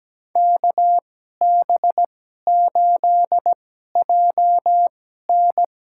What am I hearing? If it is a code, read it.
Morse code, heard as a keyed tone, about 17 wpm, spaced as for 12 wpm, 700 Hz. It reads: KB8JN
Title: USA Call Signs - 2x Character Spacing - 17wpm